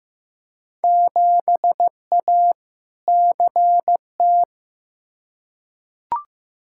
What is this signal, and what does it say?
Morse code, heard as a keyed tone, about 16 wpm, 700 Hz. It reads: 7A CT
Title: Field Day Exchanges - 15wpm